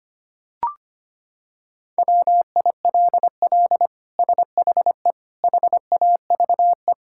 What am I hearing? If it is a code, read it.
Morse code, heard as a keyed tone, about 25 wpm, 700 Hz. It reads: WILL SHE HAVE
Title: Sentences from Top 1000 Words - Review Part 6 - 25wpm